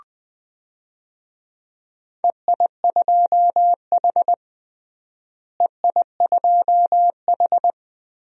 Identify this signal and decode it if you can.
Morse code, heard as a keyed tone, about 20 wpm, 700 Hz. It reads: EI2H EI2H
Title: DXCC-100 Call Signs - 4 Characters - Repeated 2x - 20wpm